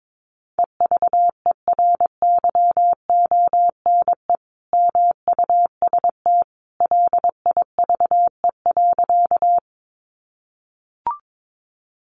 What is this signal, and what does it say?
Morse code, heard as a keyed tone, about 22 wpm, 700 Hz. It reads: EVERYONE MUST LIVE.
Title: Sentences from Top 800 Words 22wpm